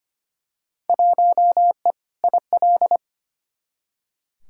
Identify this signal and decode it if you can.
Morse code, heard as a keyed tone, about 25 wpm, 700 Hz. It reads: 1E IL
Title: Field Day Exchanges - 25wpm